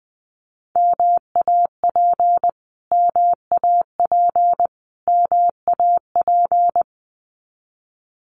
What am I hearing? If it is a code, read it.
Morse code, heard as a keyed tone, about 20 wpm, 700 Hz. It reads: MAP MAP MAP
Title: Top 500 Words Repeated 20wpm